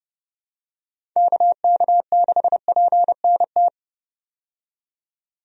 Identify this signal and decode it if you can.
Morse code, heard as a keyed tone, about 30 wpm, 700 Hz. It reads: KK6PNT
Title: USA Call Signs - 6 Characters - 30wpm